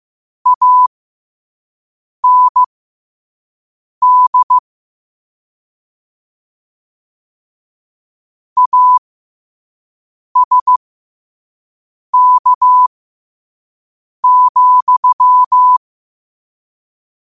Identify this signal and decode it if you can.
Morse code, heard as a keyed tone, about 15 wpm, spaced as for 5 wpm, 1000 Hz. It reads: AND ASK,
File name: qotdfiles / 5wpm